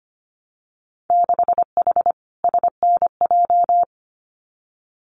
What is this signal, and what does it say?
Morse code, heard as a keyed tone, about 25 wpm, 700 Hz. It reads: 6H SNJ
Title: Winter Field Day Exchanges Encore 25wpm